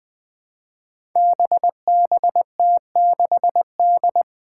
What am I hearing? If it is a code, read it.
Morse code, heard as a keyed tone, about 20 wpm, 700 Hz. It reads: BBT6D
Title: Five Challenging Letter-Number Combinations 20wpm